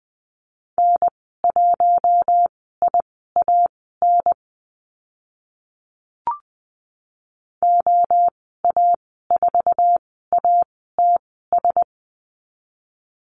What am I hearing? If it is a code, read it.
Morse code, heard as a keyed tone, about 20 wpm, spaced as for 14 wpm, 700 Hz. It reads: N1IAN OA4ATS
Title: 062 Lesson 11g - Letter h - Call Signs 20wpm